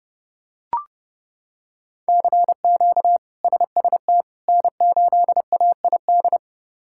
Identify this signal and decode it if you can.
Morse code, heard as a keyed tone, about 30 wpm, 700 Hz. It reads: CQ SST N8AID
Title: K1USN SST QSO Practice - 30wpm